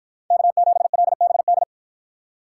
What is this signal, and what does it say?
Morse code, heard as a keyed tone, about 49 wpm, 700 Hz. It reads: K7LBD